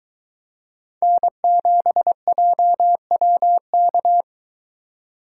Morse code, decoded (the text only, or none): N7JWK